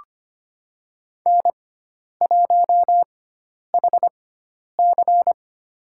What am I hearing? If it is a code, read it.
Morse code, heard as a keyed tone, about 25 wpm, spaced as for 10 wpm, 700 Hz. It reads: N1HC
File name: Lesson 16g - Letter c - Call Signs 25wpm